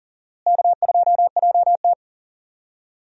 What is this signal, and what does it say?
Morse code, heard as a keyed tone, about 37 wpm, 700 Hz. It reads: K2JT
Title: USA Call Signs - 4 Characters - Repeated 3x - 40wpm